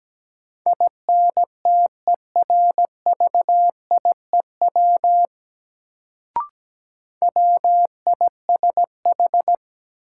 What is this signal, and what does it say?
Morse code, heard as a keyed tone, about 17 wpm, 700 Hz. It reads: INTERVIEW WISH
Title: Sentences from Top 1600 Words - New Word Drill Alt - 17wpm